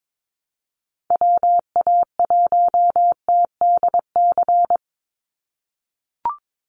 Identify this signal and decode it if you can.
Morse code, heard as a keyed tone, about 22 wpm, 700 Hz. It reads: WA1TDC
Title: USA Call Signs 22wpm